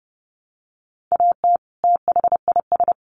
Morse code, heard as a keyed tone, about 30 wpm, 700 Hz. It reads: AT THIS
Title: Sets of 2 Words 30wpm